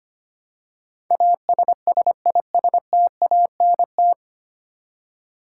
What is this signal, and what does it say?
Morse code, heard as a keyed tone, about 25 wpm, 700 Hz. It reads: ASSISTANT
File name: practice / 25wpm / 9 Letter Words - First 3 Letters Given - 25wpm